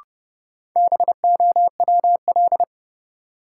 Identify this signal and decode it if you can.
Morse code, heard as a keyed tone, about 32 wpm, 700 Hz. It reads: BOWL
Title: Sentences from Top 1200 Words - New Word Drill - 30wpm